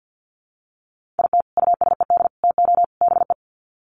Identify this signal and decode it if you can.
Morse code, heard as a keyed tone, about 50 wpm, 700 Hz. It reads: IT USED TO BE
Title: Sets of 4 Words 50wpm